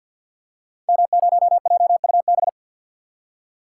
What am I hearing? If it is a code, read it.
Morse code, heard as a keyed tone, about 47 wpm, 690 Hz. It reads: M0JUB
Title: Code Groups - 5 Letter:Numbers - 50wpm